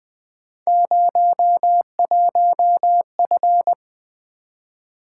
Morse code, heard as a keyed tone, about 20 wpm, 700 Hz. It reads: 01F
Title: Three Letter-Number Combinations - 20wpm